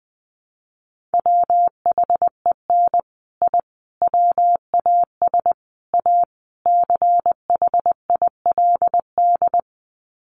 Morse code, heard as a keyed tone, about 19 wpm, 700 Hz. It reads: WHEN I WAS A CHILD
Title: Sets of 5 Words Encore 20wpm